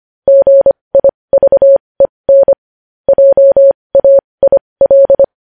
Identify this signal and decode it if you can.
Morse code, heard as a keyed tone, about 25 wpm, spaced as for 20 wpm, 550 Hz. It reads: GIVEN JAIL
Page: News Headlines in Morse Code at 20 WPM